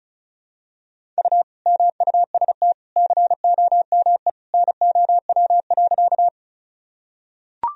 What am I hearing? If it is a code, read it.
Morse code, heard as a keyed tone, about 36 wpm, 700 Hz. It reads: U MUST COME NOW.
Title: Sentences from Top 200 Words and Abbreviations 35wpm